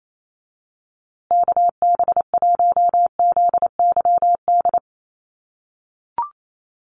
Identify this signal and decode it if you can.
Morse code, heard as a keyed tone, about 28 wpm, 700 Hz. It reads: KB1ZYD
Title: USA Call Signs Repeated 3x 28wpm